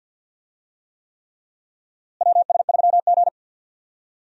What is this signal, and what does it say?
Morse code, heard as a keyed tone, about 48 wpm, 700 Hz. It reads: WS3G
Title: USA Call Signs - 4 Characters - Repeated 3x - 50wpm